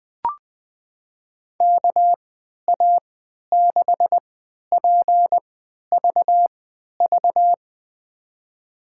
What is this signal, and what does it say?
Morse code, heard as a keyed tone, about 20 wpm, spaced as for 11 wpm, 700 Hz. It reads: KA6PVV
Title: USA Call Signs - 3x Character Spacing - 20wpm